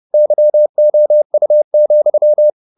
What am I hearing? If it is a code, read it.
Morse code, heard as a keyed tone, about 30 wpm, 600 Hz. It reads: YOU,
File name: the old man and the sea / the old man and the sea 1 30wpm